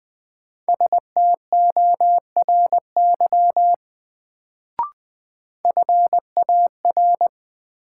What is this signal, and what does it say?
Morse code, heard as a keyed tone, about 20 wpm, 700 Hz. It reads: STORY FAR
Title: Sentences from Top 300 Words - New Word Drill - 20wpm